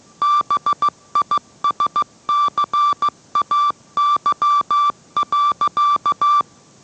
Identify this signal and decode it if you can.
Morse code, heard as a keyed tone, about 15 wpm, 1170 Hz. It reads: BISCAY.